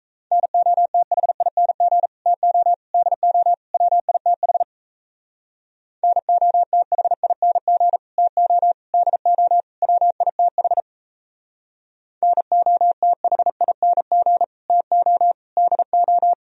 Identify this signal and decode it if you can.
Morse code, heard as a keyed tone, about 34 wpm, 700 Hz. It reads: NOTHING TO DO WITH NOTHING TO DO WITH NOTHING TO DO
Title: Speed-Racing - Sets of 4 Words 28wpm